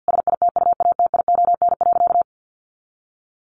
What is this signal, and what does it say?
Morse code, heard as a keyed tone, about 50 wpm, 700 Hz. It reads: SITUATION.
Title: Sentences from Top 1200 Words 50wpm